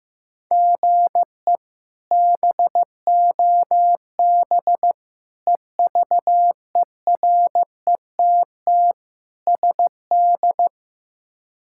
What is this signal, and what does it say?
Morse code, heard as a keyed tone, about 16 wpm, 700 Hz. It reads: GE BOB EVERETT SD
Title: K1USN SST QSO Practice - 15wpm